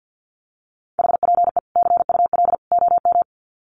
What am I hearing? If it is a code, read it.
Morse code, heard as a keyed tone, about 50 wpm, 700 Hz. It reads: HPE CUL OM
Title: Common QSO Phrases 50wpm